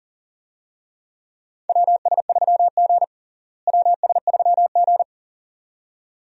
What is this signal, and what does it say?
Morse code, heard as a keyed tone, about 40 wpm, 700 Hz. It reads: WS3G WS3G
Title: USA Call Signs - 4 Characters - Repeated 2x - 40wpm